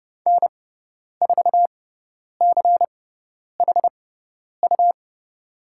Morse code, heard as a keyed tone, about 31 wpm, spaced as for 10 wpm, 700 Hz. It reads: N4CHU
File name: Lesson 17g - Letter u - Call Signs 30wpm